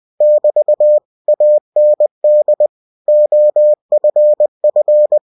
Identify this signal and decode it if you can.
Morse code, heard as a keyed tone, about 20 wpm, 600 Hz. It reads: <BT> AND OFF
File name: cw-books / winnie-the-pooh / winnie-20wpm-page-0038